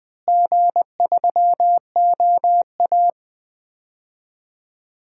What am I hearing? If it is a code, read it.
Morse code, heard as a keyed tone, about 20 wpm, 700 Hz. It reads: G3OA
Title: DXCC-100 Call Signs - 4 Characters - Repeated 2x - 20wpm